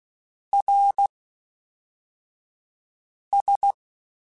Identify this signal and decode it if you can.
Morse code, heard as a keyed tone, about 16 wpm, 800 Hz. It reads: R S
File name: TH / TH mp3s / Fast TH @10